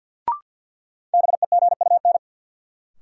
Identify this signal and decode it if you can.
Morse code, heard as a keyed tone, about 53 wpm, 700 Hz. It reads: BEGUN
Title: Sentences from Top 1000 Words - Review Part 6 - New Word Drill - 50wpm